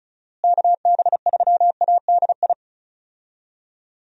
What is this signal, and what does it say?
Morse code, heard as a keyed tone, about 34 wpm, 700 Hz. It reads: KB3ADI
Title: USA Call Signs - 6 Characters - Repeated 2x - 35wpm